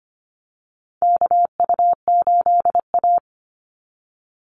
Morse code, heard as a keyed tone, about 25 wpm, 700 Hz. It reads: KU8A
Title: USA Call Signs 25wpm